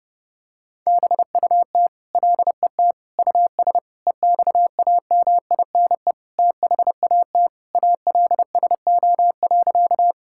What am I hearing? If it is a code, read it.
Morse code, heard as a keyed tone, about 32 wpm, 700 Hz. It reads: BUT LET US EXAMINE THAT ALSO.
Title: Sentences from Top 1600 Words 30wpm